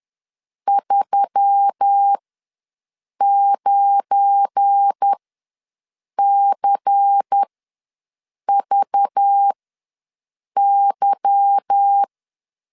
Morse code, heard as a keyed tone, about 10 wpm, spaced as for 6 wpm, 800 Hz. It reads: 39CVY
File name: LEZIONI / 8 YJ45 / 8 Y ws 10wpm 3952